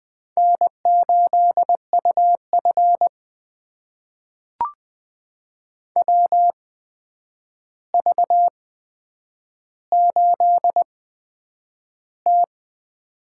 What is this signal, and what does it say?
Morse code, heard as a keyed tone, about 20 wpm, spaced as for 9 wpm, 700 Hz. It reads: N8UF WV8T
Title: USA Call Signs - 4 Characters - 8x Character Spacing - 20wpm